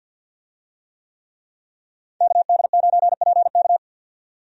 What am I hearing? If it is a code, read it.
Morse code, heard as a keyed tone, about 52 wpm, 700 Hz. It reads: KD9PK
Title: USA Call Signs - 5 Characters - 50wpm